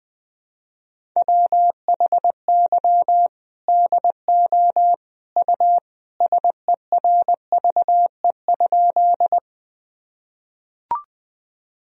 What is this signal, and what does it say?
Morse code, heard as a keyed tone, about 20 wpm, 700 Hz. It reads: WHY DO U SERVE?
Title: Sentences from Top 400 Words and Abbreviations 20wpm